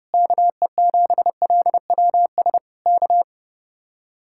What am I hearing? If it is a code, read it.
Morse code, heard as a keyed tone, about 31 wpm, 700 Hz. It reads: KE7LWS K